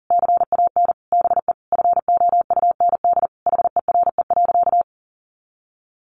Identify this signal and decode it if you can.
Morse code, heard as a keyed tone, about 40 wpm, 700 Hz. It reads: CAN BE FOUND HERE.
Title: Sentences from Top 1500 Words 40wpm